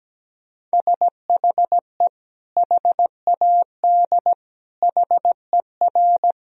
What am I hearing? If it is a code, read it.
Morse code, heard as a keyed tone, about 17 wpm, 700 Hz. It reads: SHE HAD HER
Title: Sentences from Top 1000 Words - Review Part 2 - 17wpm